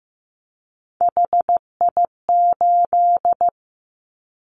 Morse code, heard as a keyed tone, about 15 wpm, 700 Hz. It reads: HI8
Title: Callsign Prefixes for Top 100 DXCC - 15wpm